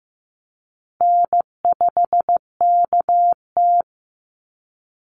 Morse code, heard as a keyed tone, about 15 wpm, 700 Hz. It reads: N5KT
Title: Easy Call Signs Repeated 3x - 15wpm